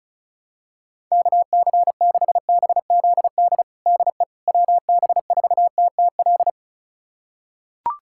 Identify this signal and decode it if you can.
Morse code, heard as a keyed tone, about 36 wpm, 700 Hz. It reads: KC6BZD DE WB4TTL
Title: USA Callsign DE Callsign - 6 Characters - 35wpm